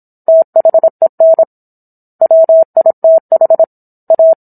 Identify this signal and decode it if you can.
Morse code, heard as a keyed tone, about 27 wpm, 650 Hz. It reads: THEN WITH A